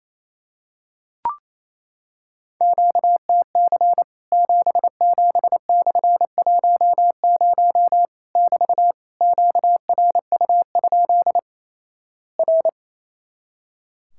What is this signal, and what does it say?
Morse code, heard as a keyed tone, about 28 wpm, 700 Hz. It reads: QTC 77/10 <BT> QRU? R
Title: Worked All Europe DX Contest - QTC Exchange - 28wpm